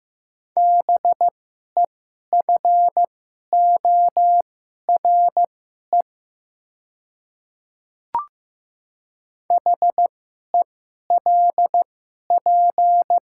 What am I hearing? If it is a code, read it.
Morse code, heard as a keyed tone, about 15 wpm, spaced as for 11 wpm, 690 Hz. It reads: BEFORE HELP
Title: 2x Character Spacing - Top 300 Words - 15wpm